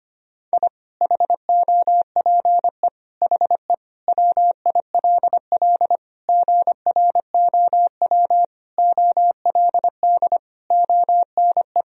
Morse code, heard as a keyed tone, about 25 wpm, 700 Hz. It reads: I HOPE HE WILL GROW OLD ONE
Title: Sentences from Top 1000 Words - Review Part 1 - 25wpm